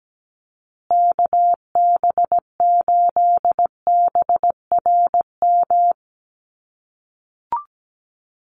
Morse code, heard as a keyed tone, about 17 wpm, 700 Hz. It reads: KB8BRM
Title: USA Call Signs Repeated 2x 17wpm